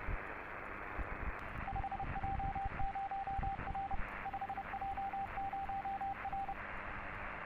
Morse code, heard as a keyed tone, about 30 wpm, 790 Hz, with radio noise. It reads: 5J0R 5J0R
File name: 5J0R - CQ 6m, CW/SSB